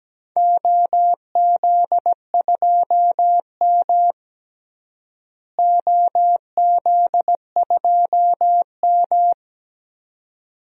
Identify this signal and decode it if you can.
Morse code, heard as a keyed tone, about 17 wpm, 700 Hz. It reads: OZ2M OZ2M
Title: DXCC-100 Call Signs - 4 Characters - Repeated 2x - 17wpm